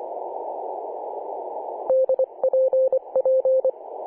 Morse code, heard as a keyed tone, about 25 wpm, 520 Hz. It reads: DPP
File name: 2013/Sep/22 2149z A35JP/H 21.010MHz CW